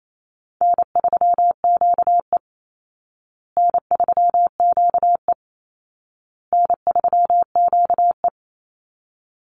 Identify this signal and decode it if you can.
Morse code, heard as a keyed tone, about 28 wpm, 700 Hz. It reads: N3QE N3QE N3QE
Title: Easy Call Signs Repeated 3x - 28wpm